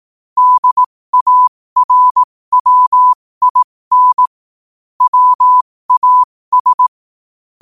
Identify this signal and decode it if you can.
Morse code, heard as a keyed tone, about 18 wpm, spaced as for 15 wpm, 1000 Hz. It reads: DARWIN WAS
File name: qotdfiles / 15wpm